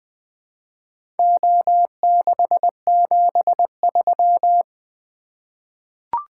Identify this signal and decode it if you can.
Morse code, heard as a keyed tone, about 20 wpm, 700 Hz. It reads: O673
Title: Four Letter-Number Combinations - 20wpm